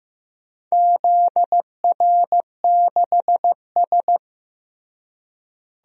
Morse code, heard as a keyed tone, about 16 wpm, 700 Hz. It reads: ZR6S
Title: DXCC-100 Call Signs - 4 Characters - Repeated 3x - 15wpm